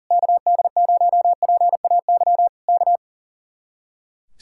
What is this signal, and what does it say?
Morse code, heard as a keyed tone, about 38 wpm, 700 Hz. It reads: KD0PAY K
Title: Speed-Racing Common QSO Phrases - 40wpm